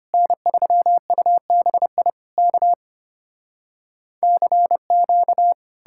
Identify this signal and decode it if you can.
Morse code, heard as a keyed tone, about 33 wpm, 700 Hz. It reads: N3UBI K CQ